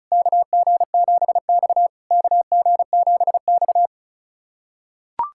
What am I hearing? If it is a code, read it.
Morse code, heard as a keyed tone, about 36 wpm, 690 Hz. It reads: KG7X KG7X
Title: SOTA QSO Practice 35wpm